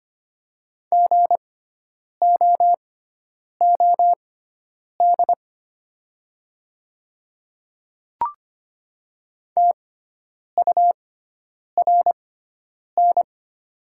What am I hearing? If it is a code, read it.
Morse code, heard as a keyed tone, about 25 wpm, spaced as for 9 wpm, 700 Hz. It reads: GOOD TURN
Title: 6x Character Spacing - Top 200 Words - 25wpm